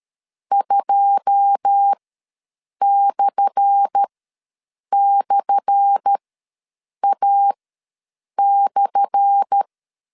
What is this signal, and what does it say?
Morse code, heard as a keyed tone, about 13 wpm, spaced as for 7 wpm, 800 Hz. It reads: 2//A/
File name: LEZIONI / 10 SIGNS / 10A SLASH ws 12wpm 3443